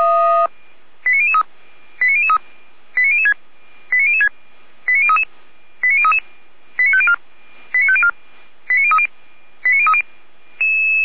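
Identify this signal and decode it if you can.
Five-tone selective call (ZVEI), heard as quick runs of five tones, tones of about 70 ms. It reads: 89003 89003 89006 89006 89033 89033 89504 89504 89030
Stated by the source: SDR-received ZVEI five-tone selective call, fire brigade Feuerwehr Orlamünde.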